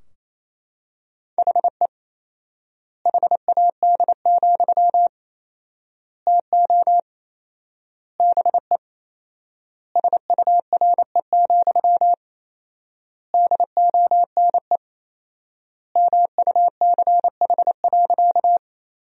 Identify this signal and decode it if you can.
Morse code, heard as a keyed tone, about 28 wpm, 700 Hz. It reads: HE HAD, TO BE SURE, DONE MUCH.
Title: 4x Word-Spacing - Sentences from Top 500 Words - 28wpm